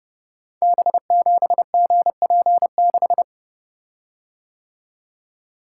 Morse code, heard as a keyed tone, about 30 wpm, 700 Hz. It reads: B7GP6
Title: Code Groups - 5 Letter:Numbers - 30wpm